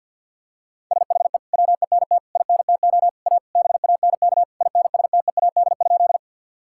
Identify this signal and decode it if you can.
Morse code, heard as a keyed tone, about 52 wpm, 700 Hz. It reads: SHE WENT INTO A BANK INSTEAD?
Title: Sentences from Top 1000 Words - Review Part 6 - 50wpm